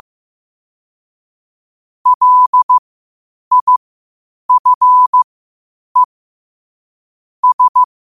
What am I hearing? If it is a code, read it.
Morse code, heard as a keyed tone, about 15 wpm, spaced as for 8 wpm, 1000 Hz. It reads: LIFE S